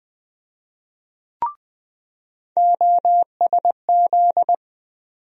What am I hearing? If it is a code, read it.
Morse code, heard as a keyed tone, about 20 wpm, 700 Hz. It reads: OSZ
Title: USA Call Sign Postfixes - 3 Letters 20wpm